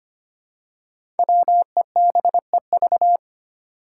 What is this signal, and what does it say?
Morse code, heard as a keyed tone, about 25 wpm, 700 Hz. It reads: WEBEV
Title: Five Challenging Letter-Number Combinations 25wpm